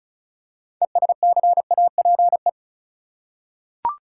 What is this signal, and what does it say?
Morse code, heard as a keyed tone, about 36 wpm, 700 Hz. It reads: ESCAPE
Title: Six Letter Words - 35wpm